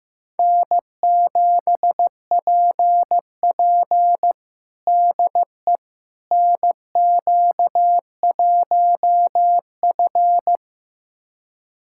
Morse code, heard as a keyed tone, about 15 wpm, 700 Hz. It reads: N7PP DE NQ1F
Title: USA Callsign DE Callsign - 4 Characters - 15wpm